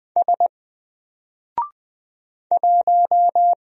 Morse code, heard as a keyed tone, about 20 wpm, 700 Hz. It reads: S 1
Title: Single Character (Letter:Number:Punctuation:Prosign) Alt 20wpm